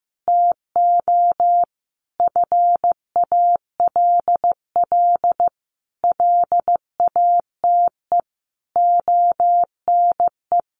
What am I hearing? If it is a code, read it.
Morse code, heard as a keyed tone, about 15 wpm, 700 Hz. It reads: TO FALL LATE ONE
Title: Sentences from Top 1500 Words 15wpm